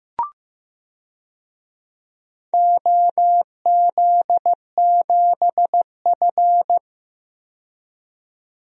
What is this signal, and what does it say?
Morse code, heard as a keyed tone, about 15 wpm, 700 Hz. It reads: OZ7F
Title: DXCC-100 Call Signs - 4 Characters - 15wpm